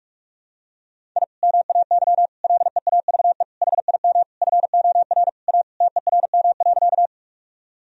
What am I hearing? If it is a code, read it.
Morse code, heard as a keyed tone, about 46 wpm, 700 Hz. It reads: I MAY LEAVE HIM FOR A TERM.
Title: Sentences from Top 1000 Words - Review Part 4 - 45wpm